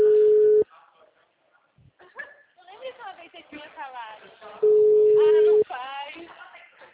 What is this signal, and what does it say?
Ringback tone (Europe, 425 Hz), with speech.
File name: out- -2010- - - .51596